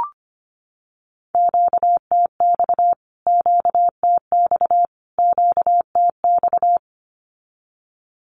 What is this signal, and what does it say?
Morse code, heard as a keyed tone, about 25 wpm, 700 Hz. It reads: QTX QTX QTX
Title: Q-Codes Repeated 3x 25wpm